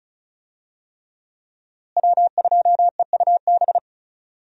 Morse code, heard as a keyed tone, about 36 wpm, 700 Hz. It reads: W2EUB
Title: USA Call Signs - 5 Characters - Repeated 3x - 35wpm